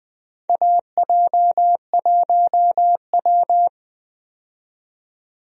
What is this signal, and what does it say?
Morse code, heard as a keyed tone, about 20 wpm, 700 Hz. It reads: AJ1W
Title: USA Call Signs - 4 Characters - 20wpm